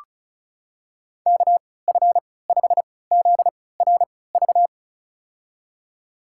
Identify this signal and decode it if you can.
Morse code, heard as a keyed tone, about 36 wpm, spaced as for 20 wpm, 690 Hz. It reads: KF5ZRV
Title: USA Call Signs - 3x Character Spacing - 35wpm